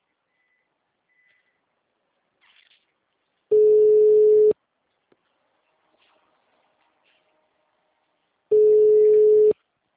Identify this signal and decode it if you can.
Ringback tone (Europe, 425 Hz).